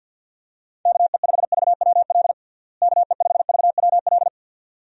Morse code, heard as a keyed tone, about 47 wpm, 700 Hz. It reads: KE5VWL KE5VWL
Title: USA Call Signs - 6 Characters - Repeated 2x - 50wpm